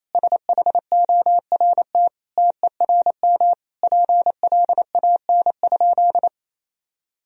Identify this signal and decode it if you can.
Morse code, heard as a keyed tone, about 28 wpm, 700 Hz. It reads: SHORT TERM PLAN?
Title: Sentences from Top 1000 Words - Review Part 4 - 28wpm